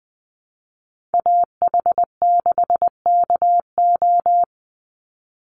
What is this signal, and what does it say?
Morse code, heard as a keyed tone, about 20 wpm, 700 Hz. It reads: AH6KO
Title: Call Signs Repeated 3x - 20wpm